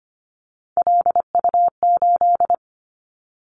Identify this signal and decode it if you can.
Morse code, heard as a keyed tone, about 25 wpm, 700 Hz. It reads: LU8
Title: Callsign Prefixes for Top 100 DXCC - 25wpm